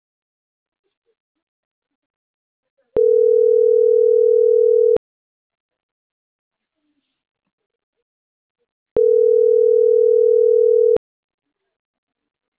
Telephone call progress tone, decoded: ringback tone

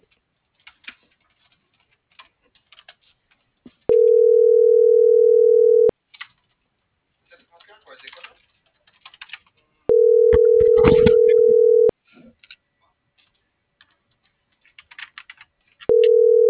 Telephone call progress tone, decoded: ringback tone

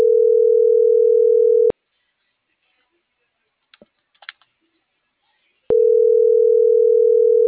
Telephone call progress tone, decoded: ringback tone

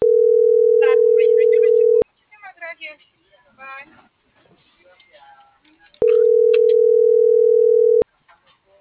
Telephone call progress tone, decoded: ringback tone

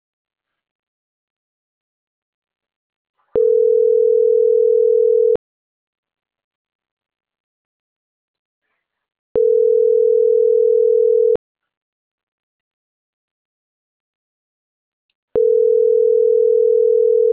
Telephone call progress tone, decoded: ringback tone

